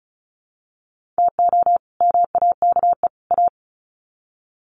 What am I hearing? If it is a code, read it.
Morse code, heard as a keyed tone, about 34 wpm, 700 Hz. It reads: TO MAKE A
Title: Sets of 3 Words 35wpm